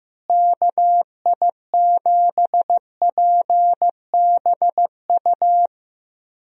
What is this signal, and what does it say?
Morse code, heard as a keyed tone, about 15 wpm, 700 Hz. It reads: KI7PBU
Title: USA Call Signs - 6 Characters - Repeated 3x - 15wpm